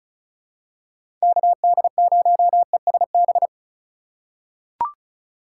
Morse code, heard as a keyed tone, about 36 wpm, 700 Hz. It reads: KD0ESB
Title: USA Call Signs - 6 Characters - 3x Character Spacing - 35wpm